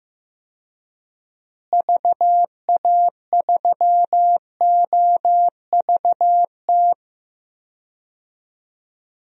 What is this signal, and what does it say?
Morse code, heard as a keyed tone, about 15 wpm, 700 Hz. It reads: VA3OVT
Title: DXCC-100 Call Signs - 6 Characters - Repeated 3x - 15wpm